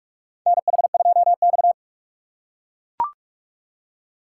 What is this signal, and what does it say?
Morse code, heard as a keyed tone, about 43 wpm, 700 Hz. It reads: NH2X
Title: USA Call Signs - 4 Characters - 3x Character Spacing - 45wpm